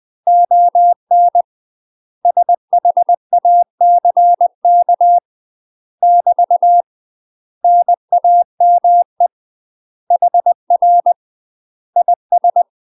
Morse code, heard as a keyed tone, about 20 wpm, 700 Hz. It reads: ON SHACK <BT> NAME HR IS
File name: UK MASTER QSOs TRAINING / 20WPM / UK MASTER QSOs 0061